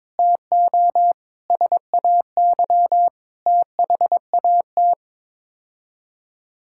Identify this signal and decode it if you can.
Morse code, heard as a keyed tone, about 22 wpm, 700 Hz. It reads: TO SAY THAT
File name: practice / 22wpm / Sets of 8 Words - 22wpm